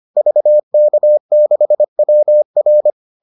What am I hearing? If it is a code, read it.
Morse code, heard as a keyed tone, about 25 wpm, 600 Hz. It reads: VK6WR